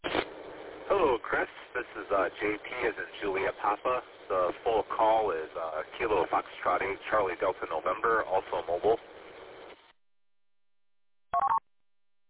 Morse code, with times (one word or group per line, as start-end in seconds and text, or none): none